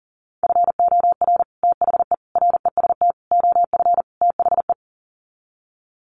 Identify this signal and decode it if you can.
Morse code, heard as a keyed tone, about 40 wpm, 700 Hz. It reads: FOR THE REST OF THE
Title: Sets of 5 Words 40wpm